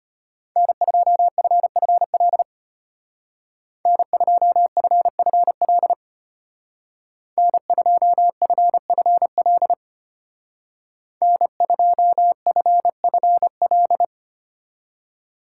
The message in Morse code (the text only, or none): N2FFL N2FFL N2FFL N2FFL